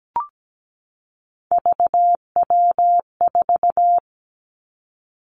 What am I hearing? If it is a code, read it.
Morse code, heard as a keyed tone, about 17 wpm, 700 Hz. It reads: VW4
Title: Three Challenging Letter-Number Combinations 17wpm